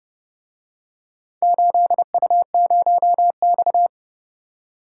Morse code, heard as a keyed tone, about 29 wpm, 700 Hz. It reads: 8U0X